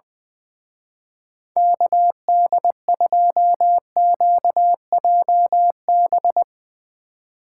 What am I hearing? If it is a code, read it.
Morse code, heard as a keyed tone, about 20 wpm, 700 Hz. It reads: KD2QJB
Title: Speed Racing - USA 2x3 Call Signs 20wpm